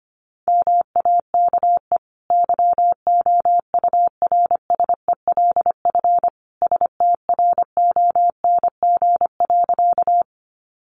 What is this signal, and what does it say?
Morse code, heard as a keyed tone, about 25 wpm, 700 Hz. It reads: MAKE YOURSELF STRONG.
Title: Sentences from Top 600 Words 25wpm